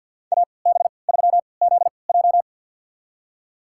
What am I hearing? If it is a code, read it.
Morse code, heard as a keyed tone, about 53 wpm, spaced as for 29 wpm, 700 Hz. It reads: AB3ZJ
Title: USA Call Signs - 5 Characters - 3x Character Spacing - 50wpm